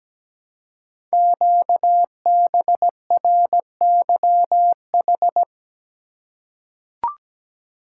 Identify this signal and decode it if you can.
Morse code, heard as a keyed tone, about 17 wpm, 700 Hz. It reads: QBRYH
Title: Code Groups - 5 Letters - 17wpm